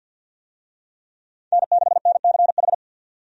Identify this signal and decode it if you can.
Morse code, heard as a keyed tone, about 50 wpm, 700 Hz. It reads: N6NCH
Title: USA Call Signs - 5 Characters - 50wpm